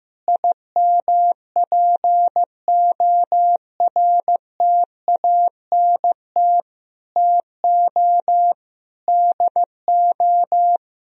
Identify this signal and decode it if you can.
Morse code, heard as a keyed tone, about 15 wpm, 700 Hz. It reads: IMPORTANT TO DO
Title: Sentences from Top 1000 Words - Review Part 4 - 15wpm